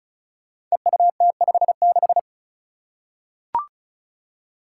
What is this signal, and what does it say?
Morse code, heard as a keyed tone, about 36 wpm, 700 Hz. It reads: EUT56
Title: Code Groups - 5 Letter:Numbers - 35wpm